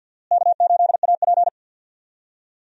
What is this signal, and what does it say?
Morse code, heard as a keyed tone, about 51 wpm, 700 Hz. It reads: K8AP
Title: USA Call Signs - 4 Characters - Repeated 2x - 50wpm